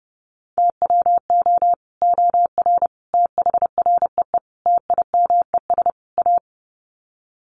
Morse code, heard as a keyed tone, about 30 wpm, 700 Hz. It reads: TWO OR THREE TIMES A
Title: Sets of 5 Words Encore2 30wpm